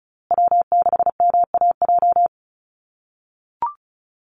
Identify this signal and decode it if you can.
Morse code, heard as a keyed tone, about 34 wpm, 700 Hz. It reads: W6MAJ
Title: USA Call Signs Repeated 3x 35wpm